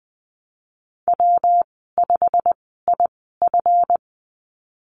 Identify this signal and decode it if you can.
Morse code, heard as a keyed tone, about 20 wpm, spaced as for 15 wpm, 700 Hz. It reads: W5IF
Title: 140 Lesson 23g - Letter f - Call Signs 20wpm